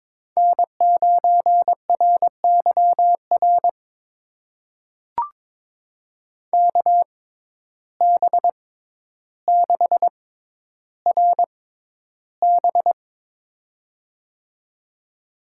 Morse code, heard as a keyed tone, about 22 wpm, spaced as for 11 wpm, 700 Hz. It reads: N9RYR KB6RB
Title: USA Call Signs - 6x Character Spacing - 22wpm